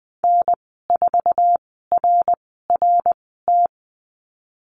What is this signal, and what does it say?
Morse code, heard as a keyed tone, about 20 wpm, spaced as for 15 wpm, 700 Hz. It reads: N4RRT
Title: 055 Lesson 10g - Letter r - Call Signs 20wpm